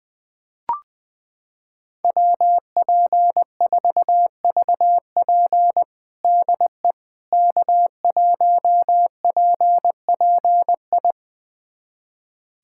Morse code, heard as a keyed tone, about 20 wpm, 700 Hz. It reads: WP4VP DE K1PPI
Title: USA Callsign DE Callsign - 5 Characters - 20wpm